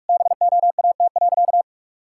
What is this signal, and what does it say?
Morse code, heard as a keyed tone, about 46 wpm, 700 Hz. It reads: BOAT.
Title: Sentences from Top 1000 Words - Review Part 6 - 45wpm